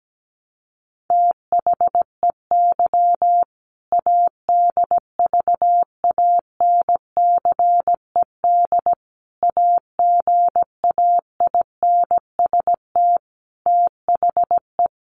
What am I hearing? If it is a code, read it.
Morse code, heard as a keyed tone, about 17 wpm, 700 Hz. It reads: THEY ADVANCED AGAINST THE
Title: Sentences from Top 1200 Words 17wpm